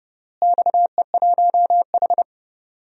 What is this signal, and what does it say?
Morse code, heard as a keyed tone, about 30 wpm, 700 Hz. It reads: XE1H
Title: DXCC-100 Call Signs - 4 Characters - Repeated 3x - 30wpm